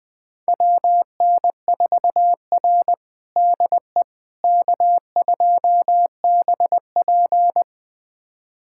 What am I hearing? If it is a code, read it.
Morse code, heard as a keyed tone, about 21 wpm, 700 Hz. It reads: WN4R DE K2BP
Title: USA Callsign DE Callsign - 4 Characters - 20wpm